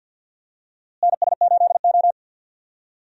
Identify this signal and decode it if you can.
Morse code, heard as a keyed tone, about 48 wpm, 700 Hz. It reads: NS8O
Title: USA Call Signs - 4 Characters - Repeated 2x - 50wpm